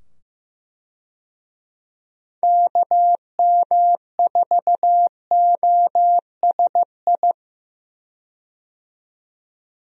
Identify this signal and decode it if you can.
Morse code, heard as a keyed tone, about 15 wpm, 700 Hz. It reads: KM4OSI
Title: USA Call Signs - 6 Characters - 15wpm